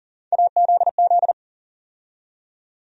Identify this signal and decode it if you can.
Morse code, heard as a keyed tone, about 40 wpm, 700 Hz. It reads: AZZ
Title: USA Call Sign Postfixes - 3 Letters 40wpm